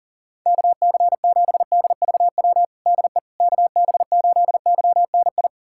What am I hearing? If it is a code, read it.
Morse code, heard as a keyed tone, about 40 wpm, 700 Hz. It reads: KC7DVW DE KB8YNI
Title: USA Callsign DE Callsign 40wpm